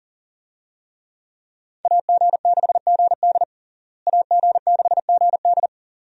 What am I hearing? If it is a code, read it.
Morse code, heard as a keyed tone, about 40 wpm, 700 Hz. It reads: AG6GD AG6GD
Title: USA Call Signs - 5 Characters - Repeated 2x - 40wpm